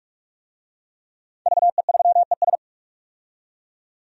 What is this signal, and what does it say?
Morse code, heard as a keyed tone, about 46 wpm, 700 Hz. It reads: VE3ES
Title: DXCC-100 Call Signs - 5 Characters - 45wpm